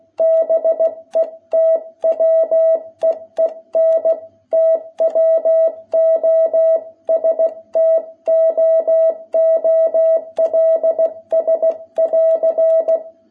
Morse code, heard as a keyed tone, about 16 wpm, 660 Hz. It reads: BETWEEN TWO STOOLS<AR>
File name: jh8cbh / 2021cw